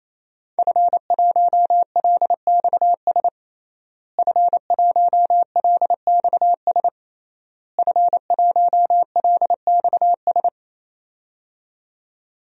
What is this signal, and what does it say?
Morse code, heard as a keyed tone, about 28 wpm, 700 Hz. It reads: F1LXS F1LXS F1LXS
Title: DXCC-100 Call Signs - 5 Characters - Repeated 3x - 28wpm